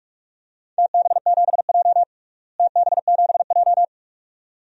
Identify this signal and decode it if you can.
Morse code, heard as a keyed tone, about 46 wpm, 700 Hz. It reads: TB7J TB7J
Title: DXCC-100 Call Signs - 4 Characters - Repeated 2x - 45wpm